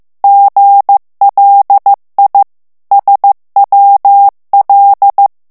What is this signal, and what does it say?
Morse code, heard as a keyed tone, about 15 wpm, 800 Hz. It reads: GLI SWL
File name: morse mp3 / ITALIEN texte clair 2 vitesse 14 mots minute